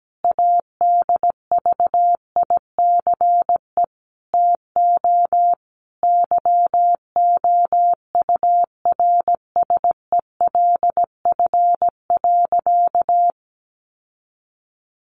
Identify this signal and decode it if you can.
Morse code, heard as a keyed tone, about 17 wpm, 700 Hz. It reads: ADVICE TO YOURSELF.
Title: Sentences from Top 1100 Words 17wpm